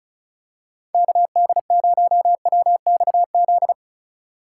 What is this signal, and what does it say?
Morse code, heard as a keyed tone, about 36 wpm, 700 Hz. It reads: KD0WXZ
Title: USA Call Signs - 6 Characters - 4x Character Spacing - 35wpm